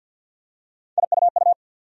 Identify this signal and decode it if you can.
Morse code, heard as a keyed tone, about 53 wpm, 700 Hz. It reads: IUU
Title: USA Call Sign Postfixes - 3 Letters 50wpm